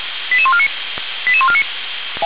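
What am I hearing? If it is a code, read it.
Five-tone selective call (ZVEI), heard as quick runs of five tones, tones of about 70 ms. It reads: 88149 88149
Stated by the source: ZVEI five-tone selective call received on SDR, from the Feuerwehr Orlamünde audio server.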